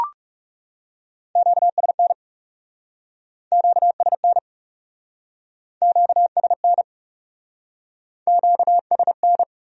Morse code, heard as a keyed tone, about 40 wpm, 700 Hz. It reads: QSN QSN QSN QSN